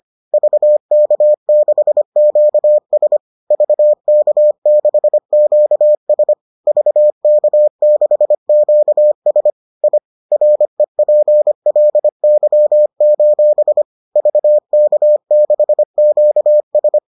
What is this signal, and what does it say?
Morse code, heard as a keyed tone, about 25 wpm, 600 Hz. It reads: VK6QS VK6QS VK6QS I REPLY: VK6QS